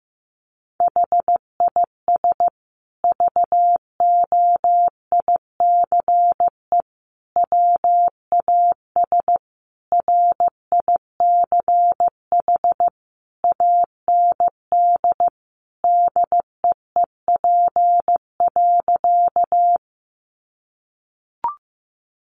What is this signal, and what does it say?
Morse code, heard as a keyed tone, about 15 wpm, 700 Hz. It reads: HIS VOICE WAS RICH AND DEEP.
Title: Sentences from Top 1100 Words 15wpm